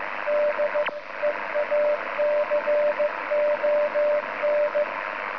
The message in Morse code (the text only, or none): D EACON